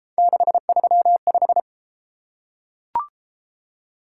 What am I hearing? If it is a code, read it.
Morse code, heard as a keyed tone, about 30 wpm, 700 Hz. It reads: 635